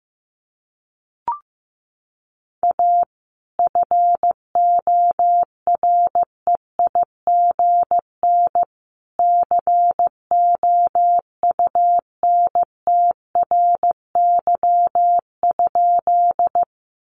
Morse code, heard as a keyed tone, about 15 wpm, 700 Hz. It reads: A FOREIGN COUNTRY?
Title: Sentences from Top 1400 Words 15wpm